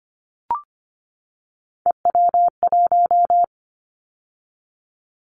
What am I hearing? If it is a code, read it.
Morse code, heard as a keyed tone, about 25 wpm, 700 Hz. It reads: EW1
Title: Callsign Prefixes for Top 100 DXCC - 25wpm